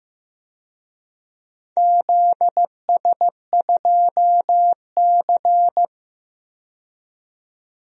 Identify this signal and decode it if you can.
Morse code, heard as a keyed tone, about 15 wpm, 700 Hz. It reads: ZS2C
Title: DXCC-100 Call Signs - 4 Characters - 15wpm